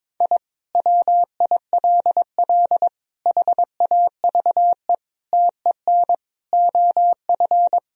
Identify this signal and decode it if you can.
Morse code, heard as a keyed tone, about 22 wpm, 700 Hz. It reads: I WILL HAVE TEN OF
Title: Sentences from Top 1000 Words - Review Part 2 - 22wpm